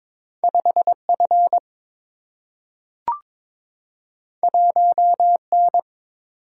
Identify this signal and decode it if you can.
Morse code, heard as a keyed tone, about 22 wpm, 700 Hz. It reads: 5 F 1 N
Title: Single-Letter Single-Number 22wpm